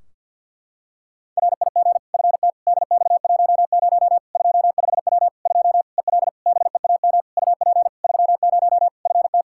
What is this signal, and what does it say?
Morse code, heard as a keyed tone, about 49 wpm, 700 Hz. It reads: RIG FT DX10 25W 2 EL BEAM UP 30 FT
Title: Practice QSO - Exchange 2 - Info - 50wpm